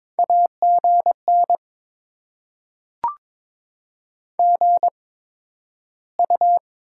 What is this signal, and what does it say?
Morse code, heard as a keyed tone, about 22 wpm, 700 Hz. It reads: A G N G U
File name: Top 20 QSO Elements - 8x Character Spacing - 22wpm